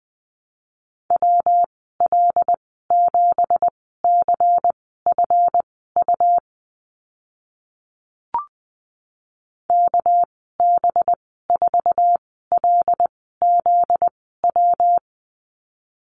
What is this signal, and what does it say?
Morse code, heard as a keyed tone, about 20 wpm, spaced as for 14 wpm, 700 Hz. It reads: WL7CFU KB4LZW
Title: 245 Lesson 39g - Letter z - Call Signs 20wpm